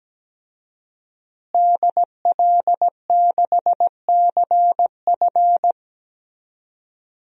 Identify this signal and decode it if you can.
Morse code, heard as a keyed tone, about 17 wpm, 700 Hz. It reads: DL6CF
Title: DXCC-100 Call Signs - 5 Characters - Repeated 2x - 17wpm